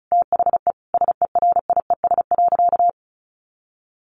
Morse code, heard as a keyed tone, about 34 wpm, 700 Hz. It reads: THE SERIES.
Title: Sentences from Top 1200 Words 35wpm